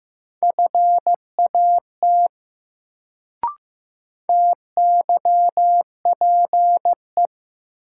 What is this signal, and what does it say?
Morse code, heard as a keyed tone, about 15 wpm, 700 Hz. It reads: FAT TYPE
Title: Sentences from Top 600 Words - New Word Drill - 15wpm